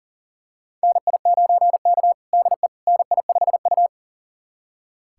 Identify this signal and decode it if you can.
Morse code, heard as a keyed tone, about 37 wpm, 700 Hz. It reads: NI9K DE NI5U
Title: USA Callsign DE Callsign - 4 Characters - 40wpm